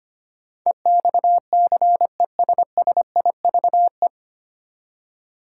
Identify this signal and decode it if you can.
Morse code, heard as a keyed tone, about 25 wpm, 700 Hz. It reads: EXCESSIVE
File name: Nine Letter Words - 25wpm